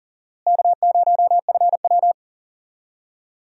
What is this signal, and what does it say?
Morse code, heard as a keyed tone, about 40 wpm, 700 Hz. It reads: K0FW
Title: USA Call Signs - 4 Characters - Repeated 2x - 40wpm